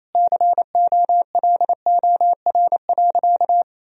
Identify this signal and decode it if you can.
Morse code, heard as a keyed tone, about 28 wpm, 700 Hz. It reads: COLOR.
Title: Sentences from Top 1000 Words - Review Part 4 - 28wpm